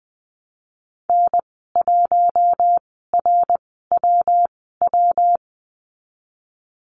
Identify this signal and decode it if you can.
Morse code, heard as a keyed tone, about 20 wpm, spaced as for 15 wpm, 700 Hz. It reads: N1RWW
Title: 140 Lesson 23g - Letter f - Call Signs 20wpm